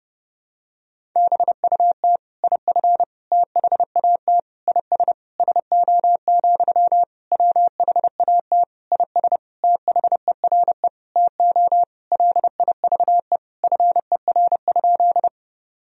Morse code, heard as a keyed tone, about 32 wpm, 700 Hz. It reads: BUT IF THAT IS SO, WHAT IS THERE TO LIVE FER?
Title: Sentences from Top 200 Words and Abbreviations 30wpm